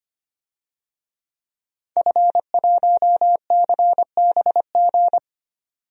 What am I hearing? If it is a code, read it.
Morse code, heard as a keyed tone, about 25 wpm, 700 Hz. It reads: F1CBG
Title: DXCC-100 Call Signs - 5 Characters - 25wpm